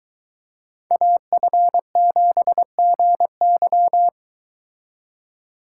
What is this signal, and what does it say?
Morse code, heard as a keyed tone, about 23 wpm, 700 Hz. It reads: AF7GY